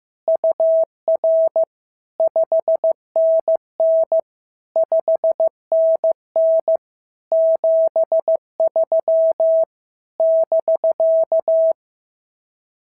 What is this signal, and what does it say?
Morse code, heard as a keyed tone, about 14 wpm, 650 Hz. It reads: UR 5NN 5NN 73 <BK>
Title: SOTA QSO Practice 15wpm